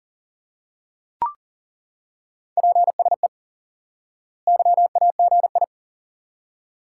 Morse code, heard as a keyed tone, about 38 wpm, 700 Hz. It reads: PSE YAGI
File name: course / instant-qso-element-course-3 / IQE Course 3 - Lesson 89 - YAGI - Two Elements - 40wpm